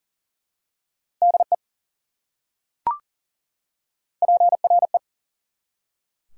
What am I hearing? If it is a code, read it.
Morse code, heard as a keyed tone, about 40 wpm, 700 Hz. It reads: DE PRE
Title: Prefixes and Suffixes 40wpm